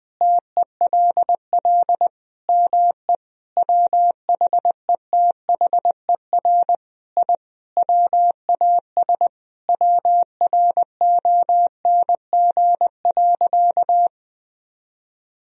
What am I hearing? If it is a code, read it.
Morse code, heard as a keyed tone, about 20 wpm, 700 Hz. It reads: TELL ME WHETHER I WAS WRONG.
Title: Sentences from Top 1000 Words - Review Part 3 - 20wpm